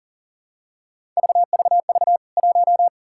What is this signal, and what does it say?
Morse code, heard as a keyed tone, about 40 wpm, 700 Hz. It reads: VVV 1
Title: K1USN SST QSO Practice 40wpm